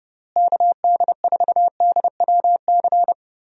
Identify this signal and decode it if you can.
Morse code, heard as a keyed tone, about 30 wpm, 700 Hz. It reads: KD4DWC
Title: USA Call Signs - 6 Characters - Repeated 3x - 30wpm